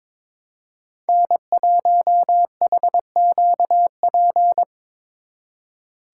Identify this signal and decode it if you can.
Morse code, heard as a keyed tone, about 26 wpm, 700 Hz. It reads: N1HQP
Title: Speed Racing - USA 1x3 Call Signs 22wpm